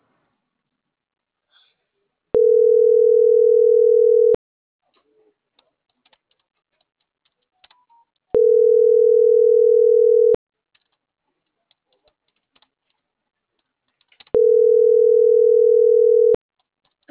Ringback tone (North America), 440 + 480 Hz.